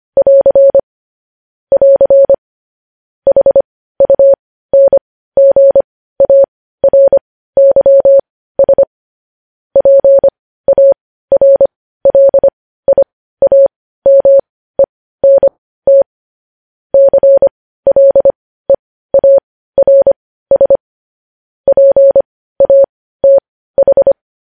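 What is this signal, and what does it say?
Morse code, heard as a keyed tone, about 25 wpm, spaced as for 15 wpm, 550 Hz. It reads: <AR> <AR> HUNGARYS PARLIAMENT CLEARS PATH
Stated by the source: Morse code news headlines at 15 WPM.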